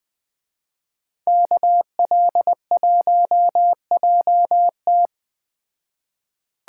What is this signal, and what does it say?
Morse code, heard as a keyed tone, about 20 wpm, 700 Hz. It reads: KL1JT